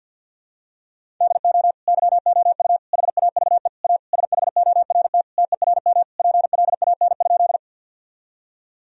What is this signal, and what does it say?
Morse code, heard as a keyed tone, about 50 wpm, 700 Hz. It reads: DO YOU HAVE A SHORT TERM PLAN?
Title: Sentences from Top 1000 Words - Review Part 4 - 50wpm